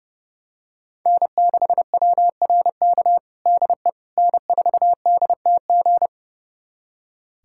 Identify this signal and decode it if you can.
Morse code, heard as a keyed tone, about 30 wpm, 700 Hz. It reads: N6WRK DE N4DTG
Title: USA Callsign DE Callsign 30wpm